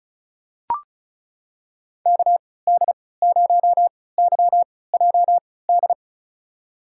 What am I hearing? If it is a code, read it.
Morse code, heard as a keyed tone, about 36 wpm, spaced as for 20 wpm, 700 Hz. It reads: KD0YJD
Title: USA Call Signs - 6 Characters - 3x Character Spacing - 35wpm